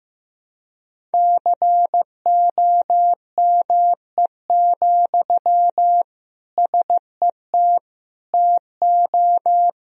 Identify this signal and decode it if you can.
Morse code, heard as a keyed tone, about 15 wpm, 700 Hz. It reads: COME, SET TO
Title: Sentences from Top 1000 Words - Review Part 1 - 15wpm